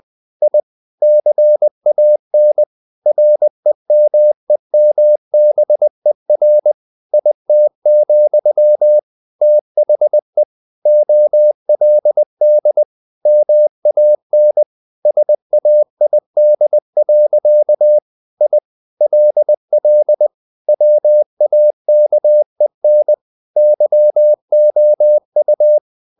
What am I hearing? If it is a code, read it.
Morse code, heard as a keyed tone, about 20 wpm, 600 Hz. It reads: I CAN REMEMBER IT, THE OLD MAN SAID. I LL WAKEN YOU
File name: the old man and the sea / the old man and the sea part 1 20wpm